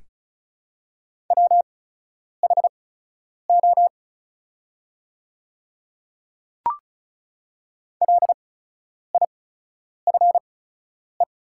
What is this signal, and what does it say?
Morse code, heard as a keyed tone, about 34 wpm, spaced as for 10 wpm, 700 Hz. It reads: WHO LIFE
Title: 8x Character Spacing - Top 500 Words - 35wpm